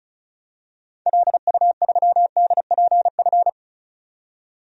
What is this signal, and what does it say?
Morse code, heard as a keyed tone, about 36 wpm, 700 Hz. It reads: LU3DPF
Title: DXCC-100 Call Signs - 6 Characters - 35wpm